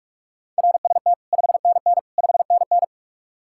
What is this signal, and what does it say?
Morse code, heard as a keyed tone, about 58 wpm, 700 Hz. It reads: RST 5NN 5NN